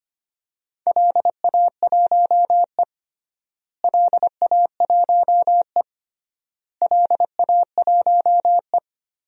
Morse code, heard as a keyed tone, about 25 wpm, 700 Hz. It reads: LA1E LA1E LA1E
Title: DXCC-100 Call Signs - 4 Characters - Repeated 3x - 25wpm